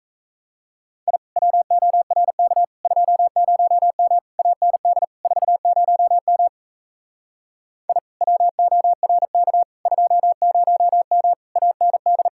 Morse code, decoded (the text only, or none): I WORK 20M AND 40M I WORK 20M AND